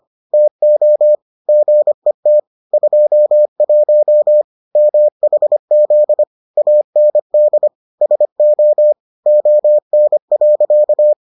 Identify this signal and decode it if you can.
Morse code, heard as a keyed tone, about 25 wpm, 600 Hz. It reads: TO GET 21 MHZ AND SO ON.